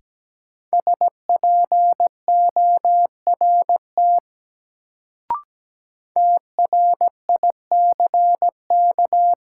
Morse code, heard as a keyed tone, about 17 wpm, 700 Hz. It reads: SPORT TRICK
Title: Sentences from Top 1500 Words - New Word Drill Alt - 17wpm